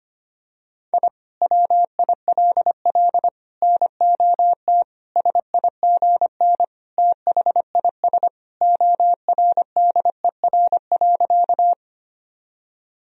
Morse code, heard as a keyed tone, about 25 wpm, 700 Hz. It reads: I WILL NOT SIGN THIS ORDER.
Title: Sentences from Top 1000 Words - Review Part 2 - 25wpm